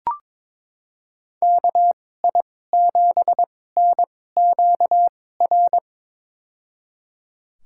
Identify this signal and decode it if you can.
Morse code, heard as a keyed tone, about 22 wpm, spaced as for 16 wpm, 700 Hz. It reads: KI7NQR
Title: USA Call Signs - 6 Characters - 2x Character Spacing - 22wpm